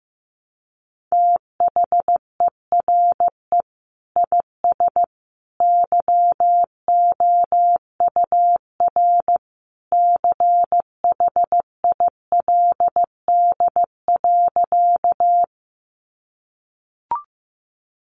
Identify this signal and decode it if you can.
Morse code, heard as a keyed tone, about 15 wpm, 700 Hz. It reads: THERE IS YOUR CHILD.
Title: Sentences from Top 800 Words 15wpm